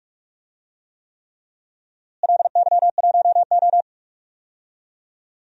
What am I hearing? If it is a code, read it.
Morse code, heard as a keyed tone, about 43 wpm, 700 Hz. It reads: LY1O
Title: DXCC-100 Call Signs - 4 Characters - Repeated 3x - 45wpm